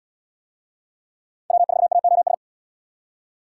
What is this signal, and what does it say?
Morse code, heard as a keyed tone, about 69 wpm, 700 Hz. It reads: W4AOS